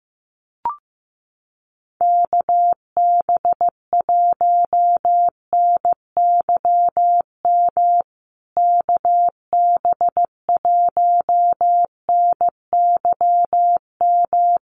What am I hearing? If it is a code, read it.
Morse code, heard as a keyed tone, about 15 wpm, 700 Hz. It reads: KB1NYM KB1NYM
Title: USA Call Signs Repeated 2x 15wpm